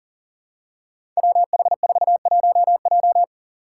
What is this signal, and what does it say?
Morse code, heard as a keyed tone, about 41 wpm, 700 Hz. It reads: WH41J
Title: Five Challenging Letter-Number Combinations 40wpm